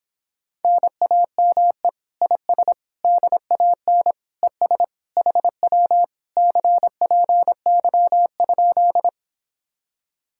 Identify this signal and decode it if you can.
Morse code, heard as a keyed tone, about 26 wpm, 700 Hz. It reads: NAME IS DAN ES HW CPY?